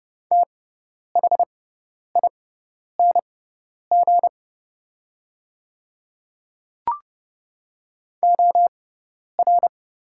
Morse code, heard as a keyed tone, about 30 wpm, spaced as for 10 wpm, 700 Hz. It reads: THING OR
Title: 6x Character Spacing - Top 100 Words - 30wpm